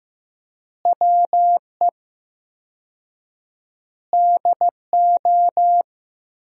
Morse code, heard as a keyed tone, about 16 wpm, 700 Hz. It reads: WE DO
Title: 4x Word-Spacing - Sentences from Top 400 Words - 15wpm